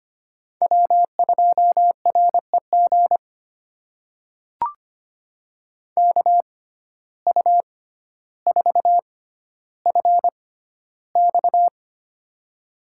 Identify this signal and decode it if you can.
Morse code, heard as a keyed tone, about 25 wpm, spaced as for 13 wpm, 700 Hz. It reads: W2REG KU4FX
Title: USA Call Signs - 5 Characters - 6x Character Spacing - 25wpm